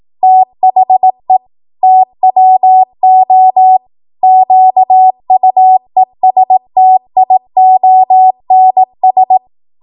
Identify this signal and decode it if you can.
Morse code, heard as a keyed tone, about 18 wpm, 750 Hz. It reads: THE TWO QUESTIONS